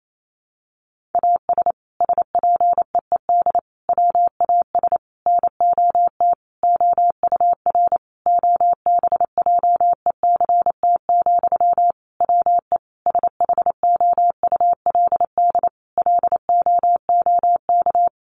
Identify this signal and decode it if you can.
Morse code, heard as a keyed tone, about 28 wpm, 700 Hz. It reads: AS SPEED WAS NOT OUR OBJECT, WE SHOULD LOOK
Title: Sentences from Top 800 Words 28wpm